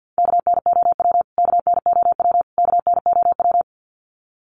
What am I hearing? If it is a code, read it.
Morse code, heard as a keyed tone, about 50 wpm, 700 Hz. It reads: KNOW KNOW KNOW
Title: Top 300 Words Repeated 50wpm